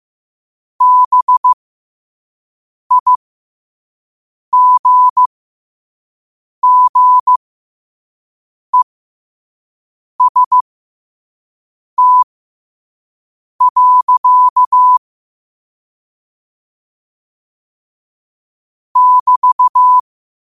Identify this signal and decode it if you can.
Morse code, heard as a keyed tone, about 15 wpm, spaced as for 5 wpm, 1000 Hz. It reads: BIGGEST. <BT>
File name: qotdfiles / 5wpm